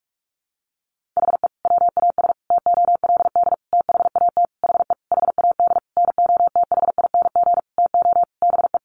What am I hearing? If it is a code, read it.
Morse code, heard as a keyed tone, about 44 wpm, 700 Hz. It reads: HE WAS TOLD THAT HE HAD NOTHING TO BE
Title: Sentences from Top 1500 Words 45wpm